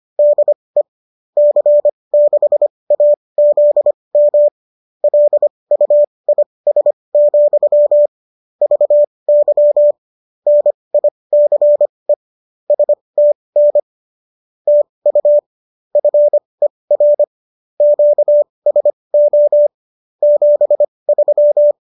Morse code, heard as a keyed tone, about 25 wpm, spaced as for 20 wpm, 590 Hz. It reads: DE C6AZM LUIS, VY NICE STN TU FER QSO 73